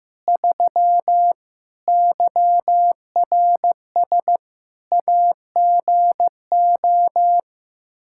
Morse code, heard as a keyed tone, about 15 wpm, 700 Hz. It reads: 3 YRS AGO
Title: Practice QSO - Exchange 3 - More Info - 15wpm